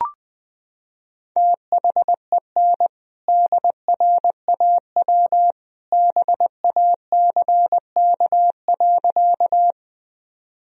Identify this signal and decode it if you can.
Morse code, heard as a keyed tone, about 21 wpm, 700 Hz. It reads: THEN DRAW BACK.
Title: Sentences from Top 1000 Words - Review Part 1 - 20wpm